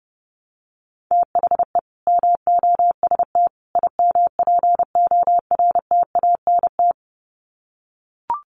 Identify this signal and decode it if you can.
Morse code, heard as a keyed tone, about 30 wpm, 700 Hz. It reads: THE MOST IMPORTANT
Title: Sets of 3 Words 30wpm